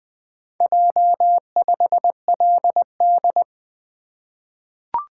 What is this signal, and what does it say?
Morse code, heard as a keyed tone, about 20 wpm, 700 Hz. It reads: J5LD
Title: Four Letter-Number Combinations - 20wpm